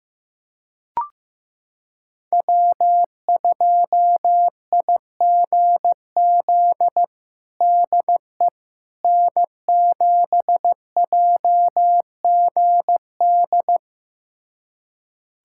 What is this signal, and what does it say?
Morse code, heard as a keyed tone, about 16 wpm, 700 Hz. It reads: W2IGZ DE N7JGD
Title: USA Callsign DE Callsign - 5 Characters - 15wpm